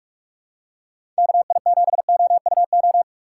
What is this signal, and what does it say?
Morse code, heard as a keyed tone, about 43 wpm, 700 Hz. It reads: KI7OUO
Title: USA Call Signs - 4x Character Spacing - 45wpm